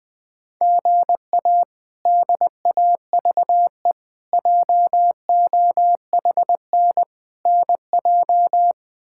Morse code, heard as a keyed tone, about 21 wpm, 700 Hz. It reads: GA DAVE JOHN NJ
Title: K1USN SST QSO Practice - 20wpm